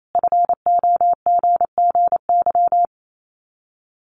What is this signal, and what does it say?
Morse code, heard as a keyed tone, about 28 wpm, 700 Hz. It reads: FOGGY
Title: Common QSO Phrases 28wpm